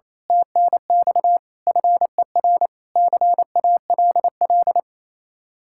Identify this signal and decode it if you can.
Morse code, heard as a keyed tone, about 26 wpm, 700 Hz. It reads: TNX FER CALL
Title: Speed-Racing Common QSO Phrases - 28wpm